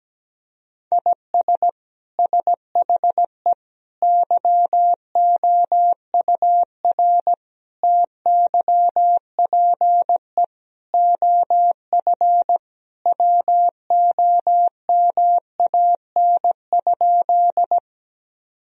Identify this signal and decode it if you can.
Morse code, heard as a keyed tone, about 17 wpm, 700 Hz. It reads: IS SHE YOUR TYPE OF WOMAN?
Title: Sentences from Top 1000 Words - Review Part 3 - 17wpm